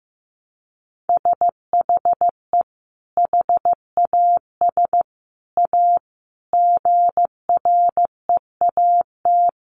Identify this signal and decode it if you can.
Morse code, heard as a keyed tone, about 15 wpm, 700 Hz. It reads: SHE HAS A GREAT
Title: Sentences from Top 1300 Words 15wpm